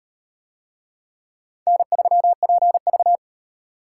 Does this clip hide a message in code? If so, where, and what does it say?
Yes: Morse code from 1.7 to 3.2 s, heard as a keyed tone, about 38 wpm, 700 Hz. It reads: N3PV